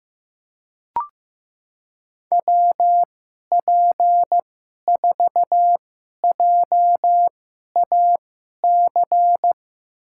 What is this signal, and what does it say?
Morse code, heard as a keyed tone, about 15 wpm, spaced as for 11 wpm, 700 Hz. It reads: WP4JAC
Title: USA Call Signs - 6 Characters - 2x Character Spacing - 15wpm